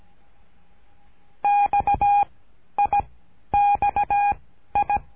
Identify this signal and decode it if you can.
Morse code, heard as a keyed tone, about 17 wpm, 820 Hz. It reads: X I X I
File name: cursos / CURSO DE TELEGRAFIA / Telegrafia - Trilha 13